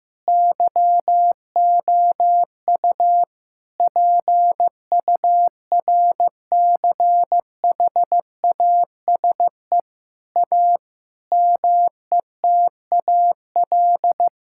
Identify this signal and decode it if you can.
Morse code, heard as a keyed tone, about 15 wpm, 700 Hz. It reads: YOU PURCHASE A METAL
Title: Sentences from Top 1000 Words - Review Part 6 - 15wpm